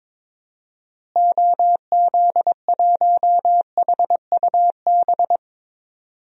Morse code, heard as a keyed tone, about 23 wpm, 700 Hz. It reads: OZ1HUB
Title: DXCC-100 Call Signs - 6 Characters - Repeated 2x - 22wpm